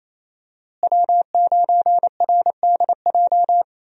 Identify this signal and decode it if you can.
Morse code, heard as a keyed tone, about 28 wpm, 700 Hz. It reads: W9RDJ
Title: USA Call Signs - 5 Characters - Repeated 2x - 28wpm